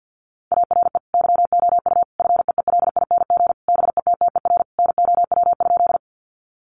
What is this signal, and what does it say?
Morse code, heard as a keyed tone, about 50 wpm, 700 Hz. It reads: ARE YOU FEELING BETTER NOW?
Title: Sentences from Top 1200 Words 50wpm